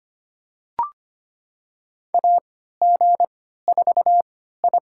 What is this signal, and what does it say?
Morse code, heard as a keyed tone, about 25 wpm, spaced as for 14 wpm, 700 Hz. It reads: AG4I
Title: USA Call Signs - 3x Character Spacing - 25wpm